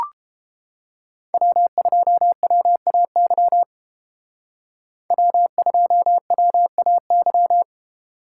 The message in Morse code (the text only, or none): W2WAY W2WAY